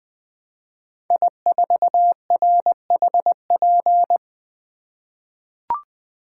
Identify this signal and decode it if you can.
Morse code, heard as a keyed tone, about 20 wpm, 700 Hz. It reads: I4RHP
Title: 2023 POTA Hunter Call Signs - 20wpm